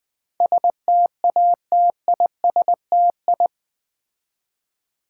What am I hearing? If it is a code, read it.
Morse code, heard as a keyed tone, about 20 wpm, 700 Hz. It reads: STATISTI
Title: Build A Word - 4 Letters At A Time - 11-Letter Words - 20wpm